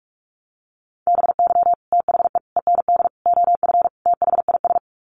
Morse code, heard as a keyed tone, about 44 wpm, 700 Hz. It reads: BY THE END OF THIS
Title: Sets of 5 Words Encore 45wpm